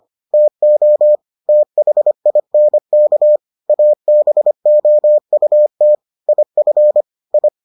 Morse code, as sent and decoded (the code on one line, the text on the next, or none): - --- / - .... .. -. -.- / .- -... --- ..- - / .. ..-. / ..
TO THINK ABOUT IF I